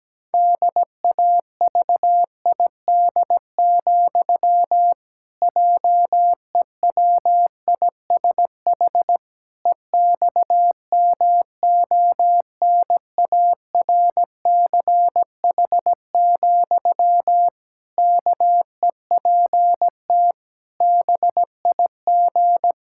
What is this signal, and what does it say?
Morse code, heard as a keyed tone, about 17 wpm, 700 Hz. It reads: DAVID, JEWISH EXMONARCH, KEPT BIG
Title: Pangram Challenge v2 - 17wpm